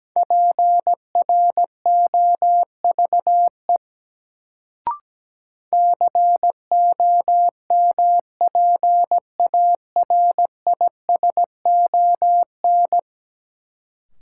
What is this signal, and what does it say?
Morse code, heard as a keyed tone, about 17 wpm, 700 Hz. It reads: PROVE COMPARISON
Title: Sentences from Top 1600 Words - New Word Drill - 17wpm